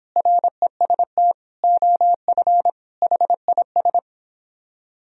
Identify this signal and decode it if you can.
Morse code, heard as a keyed tone, about 31 wpm, 700 Hz. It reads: REST OF HIS